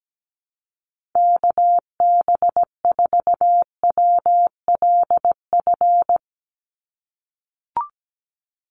Morse code, heard as a keyed tone, about 17 wpm, 700 Hz. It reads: KB4WLF
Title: Call Signs - 17wpm